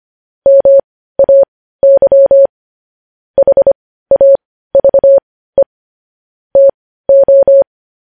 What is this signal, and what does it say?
Morse code, heard as a keyed tone, about 25 wpm, spaced as for 15 wpm, 550 Hz. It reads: MAY HAVE TO
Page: news-headlines-in-morse-code-at-15-wpm